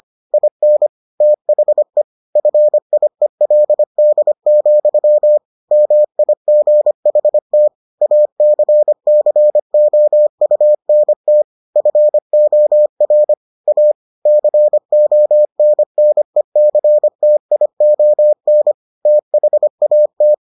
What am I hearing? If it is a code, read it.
Morse code, heard as a keyed tone, about 25 wpm, 600 Hz. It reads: IN THE FIELD, MIGHT ACCOUNT FOR A CONNECTION THAT